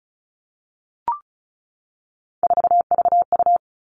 Morse code, heard as a keyed tone, about 34 wpm, 700 Hz. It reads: 4VU
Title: All Cut Numbers - 3-Digit 35wpm